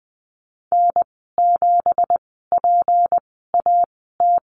Morse code, heard as a keyed tone, about 20 wpm, spaced as for 14 wpm, 700 Hz. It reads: N7PAT
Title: 167 Lesson 27g - Number 7 - Call Signs 20wpm